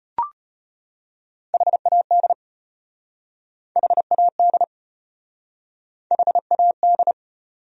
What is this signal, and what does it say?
Morse code, heard as a keyed tone, about 33 wpm, 700 Hz. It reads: HAD HAD HAD